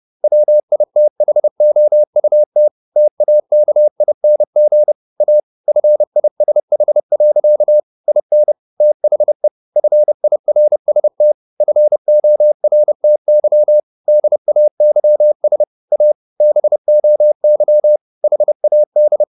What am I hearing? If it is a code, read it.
Morse code, heard as a keyed tone, about 30 wpm, 600 Hz. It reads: WITHOUT TAKING A FISH. IN THE FIRST FORTY DAYS A BOY HAD
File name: the old man and the sea / the old man and the sea 1 30wpm